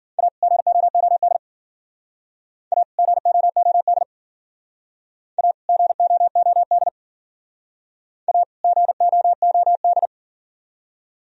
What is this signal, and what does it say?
Morse code, heard as a keyed tone, about 51 wpm, 700 Hz. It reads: A GOOD A GOOD A GOOD A GOOD